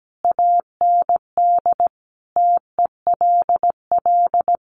Morse code, heard as a keyed tone, about 17 wpm, 700 Hz. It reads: AND TELL
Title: Sentences from Top 1200 Words 17wpm